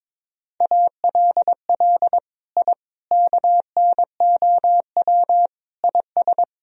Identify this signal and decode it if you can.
Morse code, heard as a keyed tone, about 22 wpm, 700 Hz. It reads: ALL I KNOW IS
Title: Sentences from Top 1000 Words - Review Part 1 - 22wpm